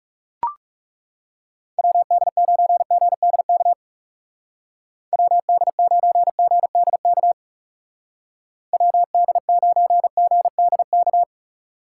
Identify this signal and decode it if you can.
Morse code, heard as a keyed tone, about 35 wpm, 700 Hz. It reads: WD9GDK WD9GDK WD9GDK
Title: Speed Racing - USA 2x3 Call Signs 30wpm